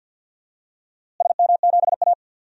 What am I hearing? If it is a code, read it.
Morse code, heard as a keyed tone, about 50 wpm, 700 Hz. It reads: SM7A
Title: DXCC-100 Call Signs - 4 Characters - Repeated 3x - 50wpm